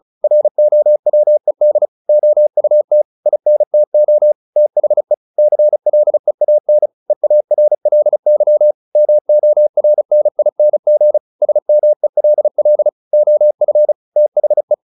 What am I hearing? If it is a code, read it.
Morse code, heard as a keyed tone, about 34 wpm, 600 Hz. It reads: ROWED OUT INTO THE CLEAN EARLY MORNING SMELL OF THE
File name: the old man and the sea / the old man and the sea 1 35wpm